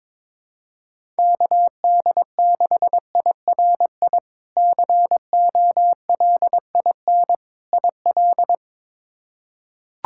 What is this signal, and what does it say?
Morse code, heard as a keyed tone, about 21 wpm, 700 Hz. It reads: KD6IRI COLIN IL
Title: K1USN SST QSO Practice - 22wpm